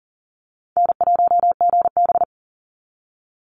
Morse code, heard as a keyed tone, about 40 wpm, 700 Hz. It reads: N1GB
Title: NA SOTA Chaser Callsigns 40wpm